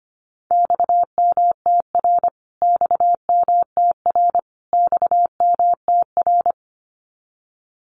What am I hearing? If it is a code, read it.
Morse code, heard as a keyed tone, about 25 wpm, 700 Hz. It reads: XMTR XMTR XMTR
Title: Abbreviations Repeated 3x - 25wpm